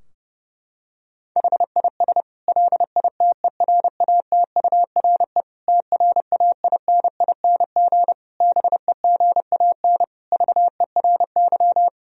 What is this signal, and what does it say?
Morse code, heard as a keyed tone, about 30 wpm, 700 Hz. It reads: HIS LITERATURE TRAINING BEGAN VERY
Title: Sentences from Top 1000 Words - Review Part 4 - 30wpm